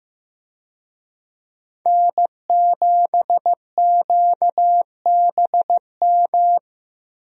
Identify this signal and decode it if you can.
Morse code, heard as a keyed tone, about 15 wpm, 700 Hz. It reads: N7QBM
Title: USA Call Signs - 5 Characters - 15wpm